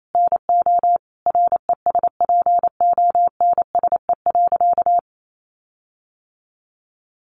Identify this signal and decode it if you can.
Morse code, heard as a keyed tone, about 28 wpm, 700 Hz. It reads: NO RESPONSE.
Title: Sentences from Top 1400 Words 28wpm